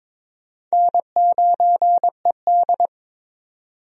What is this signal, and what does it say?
Morse code, heard as a keyed tone, about 22 wpm, 700 Hz. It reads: N9ED
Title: USA Call Signs - 4 Characters - 22wpm